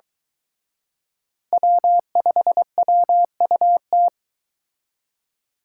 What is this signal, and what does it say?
Morse code, heard as a keyed tone, about 23 wpm, 700 Hz. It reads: W5WUT